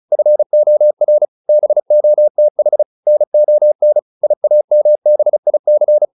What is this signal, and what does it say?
Morse code, heard as a keyed tone, about 34 wpm, 600 Hz. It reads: FOR BOTH NON IAMBIC